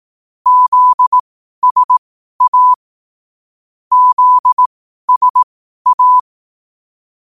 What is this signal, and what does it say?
Morse code, heard as a keyed tone, about 18 wpm, spaced as for 12 wpm, 1000 Hz. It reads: ZSA ZSA
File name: 12wpm